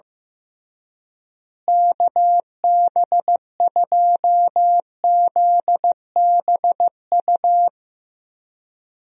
Morse code, heard as a keyed tone, about 15 wpm, 700 Hz. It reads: KB2ZBU
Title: USA Call Signs - 6 Characters - Repeated 3x - 15wpm